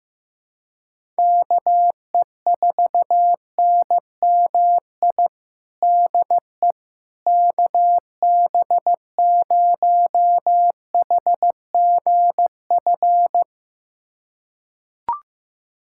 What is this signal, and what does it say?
Morse code, heard as a keyed tone, about 16 wpm, 700 Hz. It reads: KE4NMI DE KB0HGF
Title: USA Callsign DE Callsign - 6 Characters - 15wpm